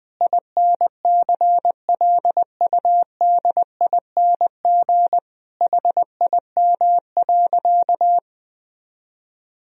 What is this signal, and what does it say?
Morse code, heard as a keyed tone, about 20 wpm, 700 Hz. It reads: INCLUDING HIM.
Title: Sentences from Top 1000 Words - Review Part 4 - 20wpm